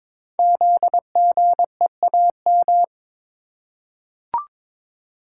Morse code, heard as a keyed tone, about 22 wpm, 700 Hz. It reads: ZGEAM
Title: Code Groups - 5 Letters - 22wpm